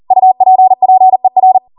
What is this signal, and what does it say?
Morse code, heard as a keyed tone, about 40 wpm, 750 Hz. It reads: UPPER
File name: Morse / 40WPM